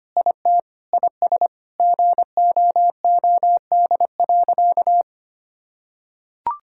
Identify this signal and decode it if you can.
Morse code, heard as a keyed tone, about 25 wpm, 700 Hz. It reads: IT IS GOOD.
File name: practice / 25wpm / Sentences with 3-Word Phrases - 25wpm